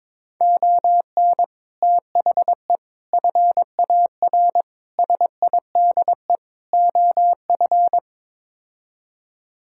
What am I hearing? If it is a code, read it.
Morse code, heard as a keyed tone, about 22 wpm, 700 Hz. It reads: ON THE FAR SIDE OF
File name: practice / 22wpm / Build A Phrase - 5-Word Phrases - 22wpm